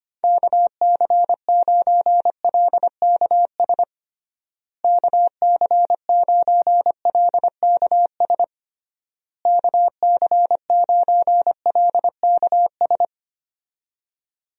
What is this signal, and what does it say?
Morse code, heard as a keyed tone, about 25 wpm, 700 Hz. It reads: KC9LKS KC9LKS KC9LKS
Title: USA Call Signs - 6 Characters - Repeated 3x - 25wpm